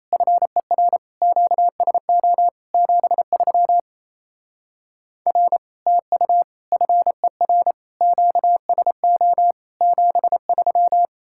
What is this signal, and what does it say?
Morse code, heard as a keyed tone, about 34 wpm, 700 Hz. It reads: FER QSO 73 R TU FER QSO 73
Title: Speed-Racing Common QSO Phrases Encore - 28wpm